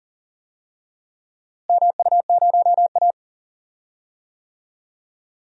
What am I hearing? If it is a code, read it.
Morse code, heard as a keyed tone, about 40 wpm, 700 Hz. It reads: MU0A
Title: DXCC-100 Call Signs - 4 Characters - Repeated 2x - 40wpm